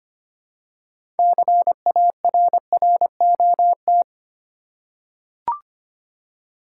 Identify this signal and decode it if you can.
Morse code, heard as a keyed tone, about 25 wpm, 700 Hz. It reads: CARROT
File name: practice / 25wpm / Easy Words with Hints - 25wpm